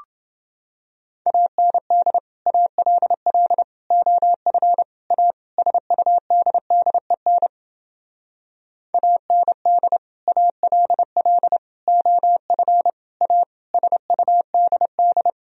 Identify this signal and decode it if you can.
Morse code, heard as a keyed tone, about 31 wpm, 700 Hz. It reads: AND ALL OF A SUDDEN AND ALL OF A SUDD